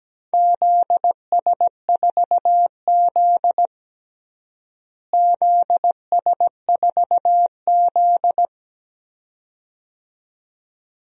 Morse code, heard as a keyed tone, about 17 wpm, 700 Hz. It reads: ZS4Z ZS4Z
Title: DXCC-100 Call Signs - 4 Characters - Repeated 2x - 17wpm